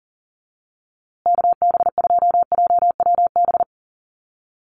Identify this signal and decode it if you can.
Morse code, heard as a keyed tone, about 40 wpm, 700 Hz. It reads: KB2JWB
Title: USA Call Signs Repeated 3x 40wpm